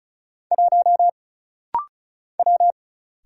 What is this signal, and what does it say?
Morse code, heard as a keyed tone, about 35 wpm, 700 Hz. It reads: 1 W
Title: Single Letter-Number - Rapid-Fire 35wpm